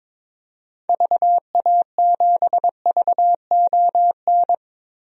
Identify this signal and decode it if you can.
Morse code, heard as a keyed tone, about 21 wpm, 700 Hz. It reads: VA7VON
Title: DXCC-100 Call Signs - 6 Characters - 22wpm